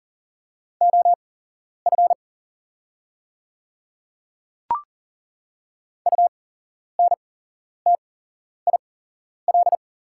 Morse code, heard as a keyed tone, about 40 wpm, spaced as for 11 wpm, 690 Hz. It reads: OF UNTIL
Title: 8x Character Spacing - Top 500 Words - 40wpm